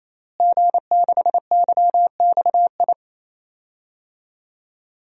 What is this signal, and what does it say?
Morse code, heard as a keyed tone, about 28 wpm, 700 Hz. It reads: G6YXI
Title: DXCC-100 Call Signs - 5 Characters - Repeated 3x - 28wpm